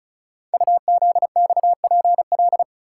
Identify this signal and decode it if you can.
Morse code, heard as a keyed tone, about 36 wpm, 700 Hz. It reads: UZXPL
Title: Code Groups - 5 Letters - 35wpm